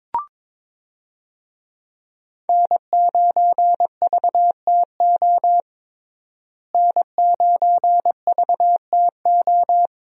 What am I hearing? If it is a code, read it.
Morse code, heard as a keyed tone, about 22 wpm, 700 Hz. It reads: N9VTO N9VTO
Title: USA Call Signs - 5 Characters - Repeated 3x - 22wpm